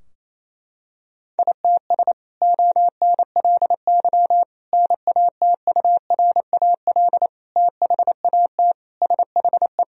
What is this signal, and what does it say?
Morse code, heard as a keyed tone, about 28 wpm, 700 Hz. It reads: ITS ONLY NATURAL THAT SHE
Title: Sentences from Top 1000 Words - Review Part 4 - 28wpm